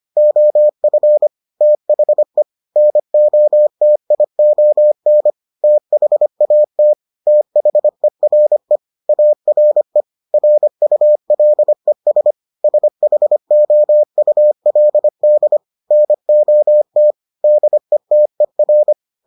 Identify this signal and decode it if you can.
Morse code, heard as a keyed tone, about 25 wpm, 600 Hz. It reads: OF THE NOTION THAT THERE ARE RULES SHOULD NOT DETER